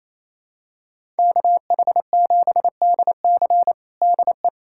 Morse code, heard as a keyed tone, about 28 wpm, 700 Hz. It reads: KH7DC DE
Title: USA Callsign DE Callsign - 5 Characters - 28wpm